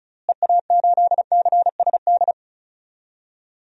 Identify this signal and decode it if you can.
Morse code, heard as a keyed tone, about 34 wpm, 700 Hz. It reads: EA8CSD
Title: DXCC-100 Call Signs - 6 Characters - Repeated 3x - 35wpm